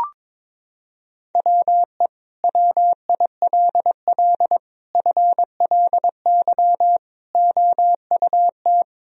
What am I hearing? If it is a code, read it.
Morse code, heard as a keyed tone, about 22 wpm, 700 Hz. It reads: WE WILL FLY OUT
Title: Sentences from Top 1000 Words - Review Part 2 - 22wpm